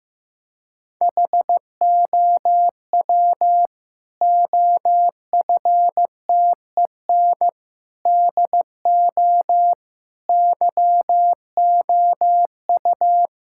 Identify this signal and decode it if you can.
Morse code, heard as a keyed tone, about 15 wpm, 700 Hz. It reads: HOW OFTEN DO YOU
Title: Sentences from Top 1000 Words - Review Part 6 - 15wpm